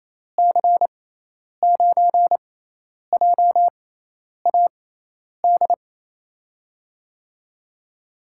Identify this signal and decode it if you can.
Morse code, heard as a keyed tone, about 28 wpm, spaced as for 10 wpm, 700 Hz. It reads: C9JAD
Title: USA Call Signs - 6 Characters - 6x Character Spacing - 28wpm